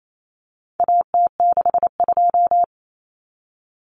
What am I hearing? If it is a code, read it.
Morse code, heard as a keyed tone, about 28 wpm, 700 Hz. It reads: AT62
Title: Common Cut Numbers - 4-Digit 28wpm